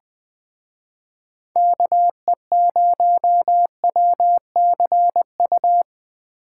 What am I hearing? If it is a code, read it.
Morse code, heard as a keyed tone, about 21 wpm, 700 Hz. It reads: KE0WCU